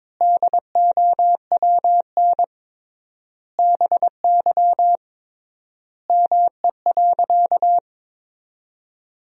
Morse code, heard as a keyed tone, about 22 wpm, 700 Hz. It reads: DOWN BY ME.
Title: 3x Word-Spacing - Sentences from Top 500 Words - 22wpm